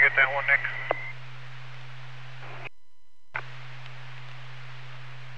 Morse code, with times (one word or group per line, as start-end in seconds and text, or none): none